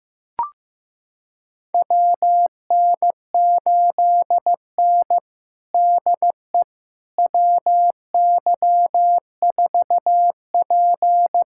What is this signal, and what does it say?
Morse code, heard as a keyed tone, about 15 wpm, 700 Hz. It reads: WN8N DE WY4P
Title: USA Callsign DE Callsign - 4 Characters - 15wpm